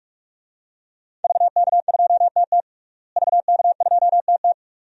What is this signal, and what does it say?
Morse code, heard as a keyed tone, about 46 wpm, 700 Hz. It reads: VK2TT VK2TT
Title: DXCC-100 Call Signs - 5 Characters - Repeated 3x - 45wpm